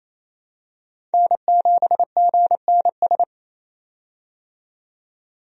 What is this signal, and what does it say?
Morse code, heard as a keyed tone, about 27 wpm, 700 Hz. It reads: N7GNS
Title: USA Call Signs - 5 Characters - 28wpm